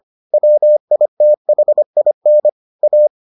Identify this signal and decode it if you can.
Morse code, heard as a keyed tone, about 25 wpm, 600 Hz. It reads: WITHIN A